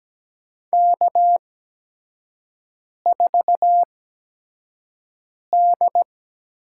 Morse code, heard as a keyed tone, about 17 wpm, spaced as for 5 wpm, 700 Hz. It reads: K4D
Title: USA Call Signs - 8x Character Spacing - 17wpm